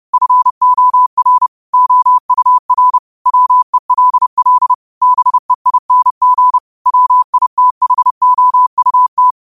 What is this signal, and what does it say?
Morse code, heard as a keyed tone, about 30 wpm, 1000 Hz. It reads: FOR OUR WELL BEING WITHOUT
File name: qotdfiles / 30wpm